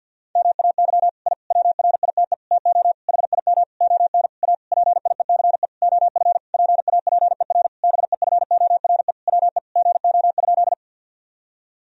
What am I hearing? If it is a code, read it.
Morse code, heard as a keyed tone, about 50 wpm, 700 Hz. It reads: MAY I WRITE TO HIM ON A PIECE OF PAPER BEFORE WE GO?
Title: Sentences from Top 1000 Words - Review Part 2 - 50wpm